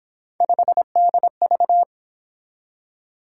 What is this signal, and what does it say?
Morse code, heard as a keyed tone, about 28 wpm, 700 Hz. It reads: 5DV